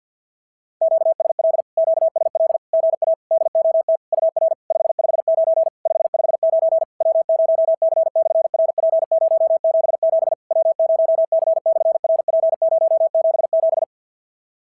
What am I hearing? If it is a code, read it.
Morse code, heard as a keyed tone, about 50 wpm, 650 Hz. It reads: QSL QSL GA DOT UR 559 559 W0C/RP077 W0C/RP077
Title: SOTA QSO Practice 50wpm